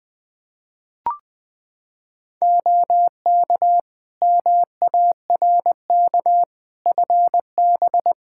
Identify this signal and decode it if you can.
Morse code, heard as a keyed tone, about 20 wpm, 700 Hz. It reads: OK MARK FB
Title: Practice QSO - Exchange 2 - Info - 20wpm